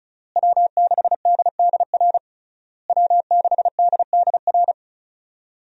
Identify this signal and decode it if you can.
Morse code, heard as a keyed tone, about 36 wpm, 700 Hz. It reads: W6DDR W6DDR
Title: USA Call Signs - 5 Characters - Repeated 3x - 35wpm